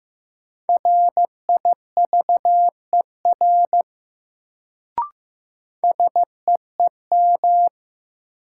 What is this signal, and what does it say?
Morse code, heard as a keyed tone, about 16 wpm, 700 Hz. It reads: RIVER SEEM
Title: Sentences from Top 300 Words - New Word Drill - 15wpm